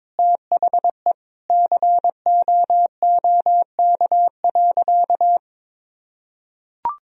Morse code, heard as a keyed tone, about 22 wpm, 700 Hz. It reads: THE COOK.
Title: Sentences from Top 1000 Words - Review Part 4 - 22wpm